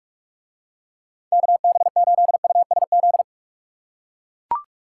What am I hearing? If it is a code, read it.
Morse code, heard as a keyed tone, about 41 wpm, 700 Hz. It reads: KB8USZ